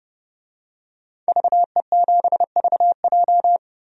Morse code, heard as a keyed tone, about 32 wpm, 700 Hz. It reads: VE7VJ
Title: DXCC-100 Call Signs - 5 Characters - 30wpm